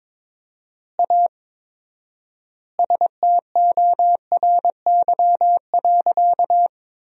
Morse code, heard as a keyed tone, about 22 wpm, 700 Hz. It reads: A STORY.
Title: 4x Word-Spacing - Sentences from Top 300 Words - 22wpm